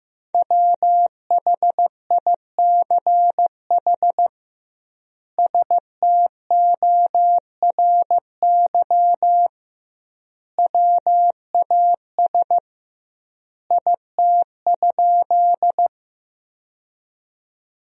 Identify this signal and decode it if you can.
Morse code, heard as a keyed tone, about 15 wpm, 700 Hz. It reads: WHICH STORY WAS IT?
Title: 2x Word-Spacing - Sentences from Top 300 Words - 15wpm